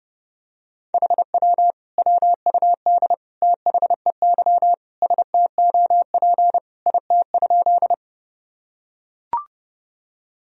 Morse code, heard as a keyed tone, about 30 wpm, 700 Hz. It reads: HW WUD THEY STOP IT?
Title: Sentences from Top 300 Words and Abbreviations 30wpm